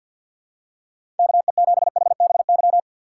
Morse code, heard as a keyed tone, about 47 wpm, 700 Hz. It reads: KE7HBY
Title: USA Call Signs - 6 Characters - 4x Character Spacing - 50wpm